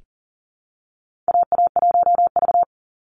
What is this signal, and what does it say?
Morse code, heard as a keyed tone, about 40 wpm, 700 Hz. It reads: AA1V
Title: Call Signs Repeated 2x - 40wpm